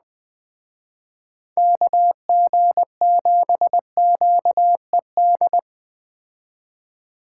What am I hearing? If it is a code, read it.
Morse code, heard as a keyed tone, about 20 wpm, 700 Hz. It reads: KG7QED
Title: Speed Racing - USA 2x3 Call Signs 17wpm